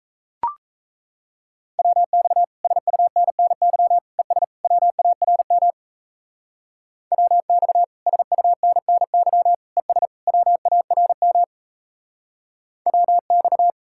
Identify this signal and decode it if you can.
Morse code, heard as a keyed tone, about 34 wpm, 700 Hz. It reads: WX SUNNY ES WARM WX SUNNY ES WARM WX
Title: Speed-Racing Common QSO Phrases Encore - 28wpm